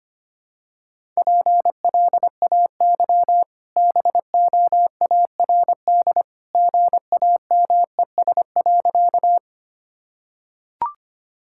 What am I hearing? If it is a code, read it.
Morse code, heard as a keyed tone, about 25 wpm, 690 Hz. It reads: PLAY BOARD GAMES.
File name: Easy Sentences with Hints - 25wpm